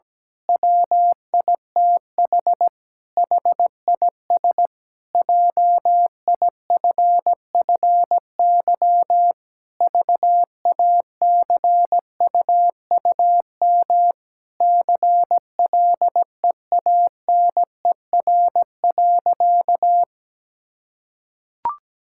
Morse code, heard as a keyed tone, about 17 wpm, 700 Hz. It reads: WITH HIS JIFFY VACUUM CLEANER.
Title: Pangram Challenge v2 - 17wpm